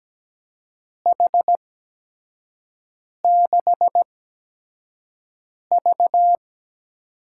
Morse code, heard as a keyed tone, about 17 wpm, spaced as for 5 wpm, 700 Hz. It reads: H6V
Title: USA Call Signs - 8x Character Spacing - 17wpm